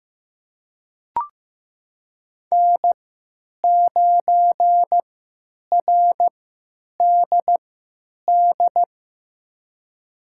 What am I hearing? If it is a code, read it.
Morse code, heard as a keyed tone, about 15 wpm, spaced as for 8 wpm, 700 Hz. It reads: N9RDD
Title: USA Call Signs - 5 Characters - 3x Character Spacing - 15wpm